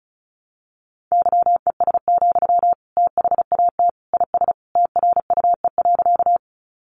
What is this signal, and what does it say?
Morse code, heard as a keyed tone, about 34 wpm, 700 Hz. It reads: YES, THAT IS TRUE.
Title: Sentences from Top 500 Words 35wpm